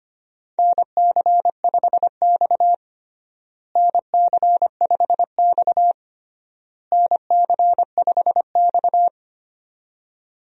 Morse code, heard as a keyed tone, about 25 wpm, 700 Hz. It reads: NC5X NC5X NC5X
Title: USA Call Signs - 4 Characters - Repeated 3x - 25wpm